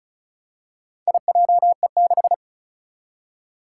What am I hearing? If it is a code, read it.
Morse code, heard as a keyed tone, about 36 wpm, 700 Hz. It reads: IJE6
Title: Four Challenging Letter-Number Combinations 35wpm